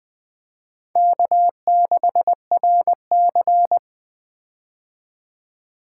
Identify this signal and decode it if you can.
Morse code, heard as a keyed tone, about 20 wpm, 700 Hz. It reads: K6RC
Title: K1USN SST QSO Practice - 20wpm